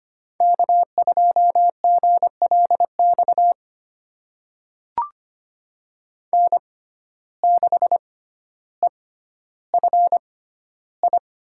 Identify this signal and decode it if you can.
Morse code, heard as a keyed tone, about 25 wpm, spaced as for 13 wpm, 700 Hz. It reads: K2GLX N6EFI
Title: USA Call Signs - 5 Characters - 6x Character Spacing - 25wpm